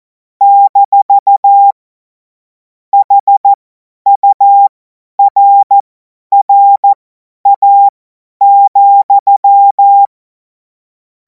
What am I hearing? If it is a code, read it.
Morse code, heard as a keyed tone, about 14 wpm, spaced as for 10 wpm, 800 Hz. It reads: - HURRA,
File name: Morsekurs / Witze / Witz 09. Juni 2026 50 F70